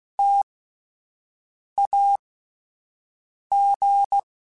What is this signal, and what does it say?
Morse code, heard as a keyed tone, about 16 wpm, spaced as for 5 wpm, 800 Hz. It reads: TAG